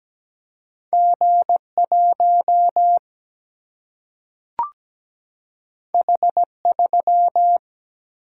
Morse code, heard as a keyed tone, about 17 wpm, 700 Hz. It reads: G1 H3
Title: Single-Letter Single-Number 17wpm